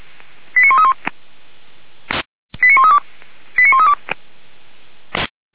Five-tone selective call (ZVEI), heard as quick runs of five tones, tones of about 80 ms. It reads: 89131 89132 89132
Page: /464/SDR/ZVEI/89110/